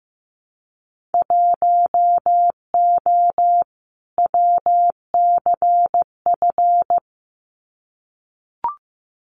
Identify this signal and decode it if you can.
Morse code, heard as a keyed tone, about 15 wpm, 700 Hz. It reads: 1O WCF
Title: Winter Field Day Exchanges 15wpm